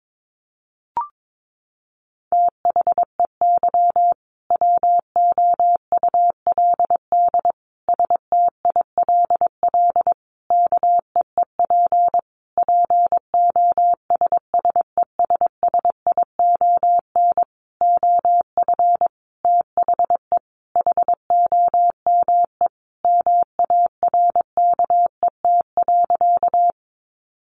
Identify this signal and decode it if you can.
Morse code, heard as a keyed tone, about 22 wpm, 700 Hz. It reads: THEY WOULD STILL KEEP POSSESSION OF THE HOME MARKET.
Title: Sentences from Top 1400 Words 22wpm